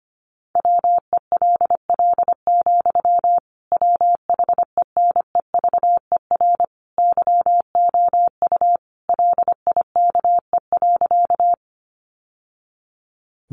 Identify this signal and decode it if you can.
Morse code, heard as a keyed tone, about 25 wpm, 700 Hz. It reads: WELL, WHENEVER YOU LIKE.
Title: Sentences from Top 1200 Words 25wpm